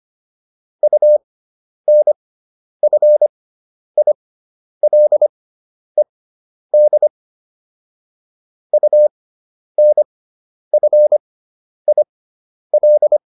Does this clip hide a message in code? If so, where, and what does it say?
Yes: Morse code from 0.8 to 13.2 s, heard as a keyed tone, about 25 wpm, spaced as for 10 wpm, 600 Hz. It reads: UNFILED UNFIL